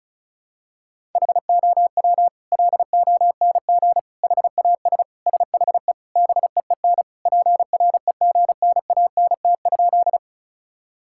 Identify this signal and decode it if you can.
Morse code, heard as a keyed tone, about 36 wpm, 700 Hz. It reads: HOW LONG HAS SHE BEEN PREGNANT?
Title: Sentences from Top 1000 Words - Review Part 6 - 35wpm